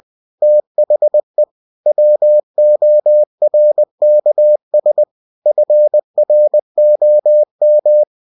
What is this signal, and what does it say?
Morse code, heard as a keyed tone, about 20 wpm, 600 Hz. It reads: THE WORKS FROM
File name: cw-books / winnie-the-pooh / winnie-20wpm-page-0112